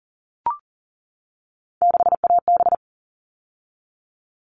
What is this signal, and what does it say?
Morse code, heard as a keyed tone, about 40 wpm, 700 Hz. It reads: 6AB
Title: All Cut Numbers - 3-Digit 40wpm